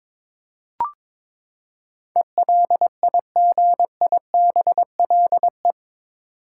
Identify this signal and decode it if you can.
Morse code, heard as a keyed tone, about 22 wpm, 700 Hz. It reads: ELIGIBLE